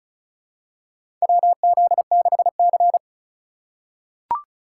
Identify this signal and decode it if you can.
Morse code, heard as a keyed tone, about 36 wpm, 700 Hz. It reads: WZ6C
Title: USA Call Signs - 3x Character Spacing - 35wpm